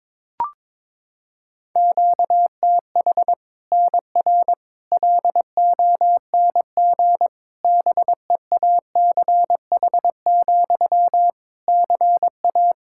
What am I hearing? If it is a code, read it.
Morse code, heard as a keyed tone, about 22 wpm, 700 Hz. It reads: QTH NR LONG BEACH, CA
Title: K1USN SST QSO Practice 22wpm